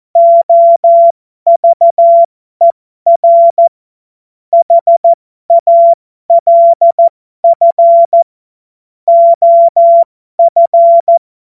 Morse code, heard as a keyed tone, about 14 wpm, spaced as for 12 wpm, 680 Hz. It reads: OVER HALF OF
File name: Morsekurs / Witze